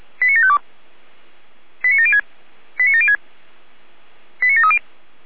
Five-tone selective call (ZVEI), heard as quick runs of five tones, tones of about 70 ms. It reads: 89752 89706 89706 89730